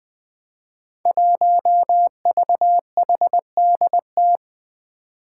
Morse code, heard as a keyed tone, about 20 wpm, 700 Hz. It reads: 1VHDT
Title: Five Challenging Letter-Number Combinations 20wpm